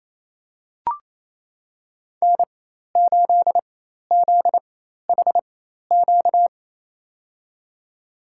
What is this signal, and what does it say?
Morse code, heard as a keyed tone, about 28 wpm, spaced as for 13 wpm, 700 Hz. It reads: N8ZHQ
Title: USA Call Signs - 5 Characters - 4x Character Spacing - 28wpm